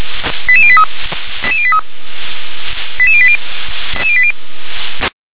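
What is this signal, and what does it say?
Five-tone selective call (ZVEI), heard as quick runs of five tones, tones of about 70 ms. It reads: 88083 88083 88080
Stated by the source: ZVEI five-tone selective call received on SDR.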